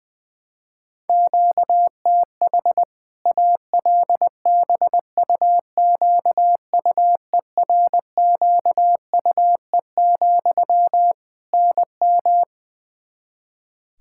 Morse code, heard as a keyed tone, about 20 wpm, 700 Hz. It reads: QTH ALBUQUERQUE, NM
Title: Practice QSO - Exchange 1 - Beginning - 20wpm